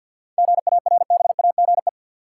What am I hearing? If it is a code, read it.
Morse code, heard as a keyed tone, about 52 wpm, 700 Hz. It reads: GARBAGE
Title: Seven Letter Words - 50wpm